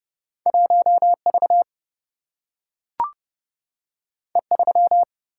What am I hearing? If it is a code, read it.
Morse code, heard as a keyed tone, about 29 wpm, 700 Hz. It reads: 1V E3
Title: Single-Letter Single-Number 30wpm